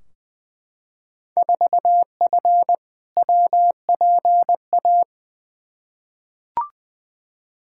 Morse code, heard as a keyed tone, about 20 wpm, 700 Hz. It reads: 4F WPA
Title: Field Day Exchanges - 20wpm